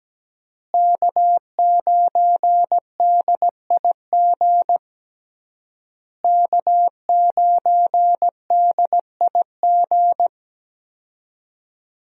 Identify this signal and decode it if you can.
Morse code, heard as a keyed tone, about 17 wpm, 700 Hz. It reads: K9DIG K9DIG
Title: USA Call Signs - 5 Characters - Repeated 3x - 17wpm